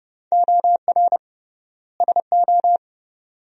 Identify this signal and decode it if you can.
Morse code, heard as a keyed tone, about 30 wpm, 700 Hz. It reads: OR SO
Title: 3x Word-Spacing - Sentences from Top 100 Words - 30wpm